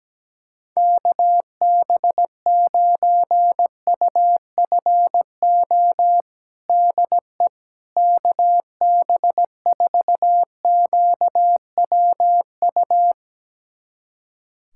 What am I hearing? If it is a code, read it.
Morse code, heard as a keyed tone, about 17 wpm, 700 Hz. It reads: KB9UFO DE KB4QWU
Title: USA Callsign DE Callsign - 6 Characters - 17wpm